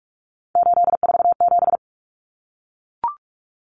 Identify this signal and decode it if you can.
Morse code, heard as a keyed tone, about 44 wpm, 700 Hz. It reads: 847
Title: Numbers - 3-Digit 45wpm